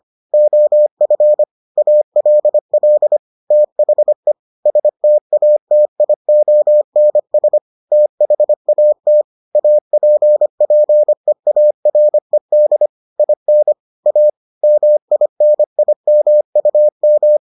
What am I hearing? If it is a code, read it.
Morse code, heard as a keyed tone, about 25 wpm, 600 Hz. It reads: OF ALL THE STATIONS THAT APPEARED IN A MINIMUM